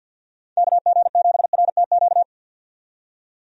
Morse code, heard as a keyed tone, about 46 wpm, 700 Hz. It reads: KG7RTQ
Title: USA Call Signs - 6 Characters - 50wpm